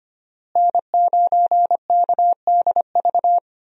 Morse code, heard as a keyed tone, about 25 wpm, 700 Hz. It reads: N9KDV
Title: USA Call Signs - 5 Characters - Repeated 2x - 25wpm